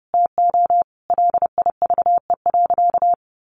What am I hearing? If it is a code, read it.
Morse code, heard as a keyed tone, about 30 wpm, 700 Hz. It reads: TO LIVE.
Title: Sentences from Top 300 Words 30wpm